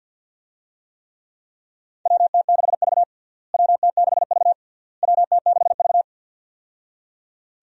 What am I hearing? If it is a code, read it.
Morse code, heard as a keyed tone, about 53 wpm, 700 Hz. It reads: WT6V WT6V WT6V
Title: USA Call Signs - 4 Characters - Repeated 3x - 50wpm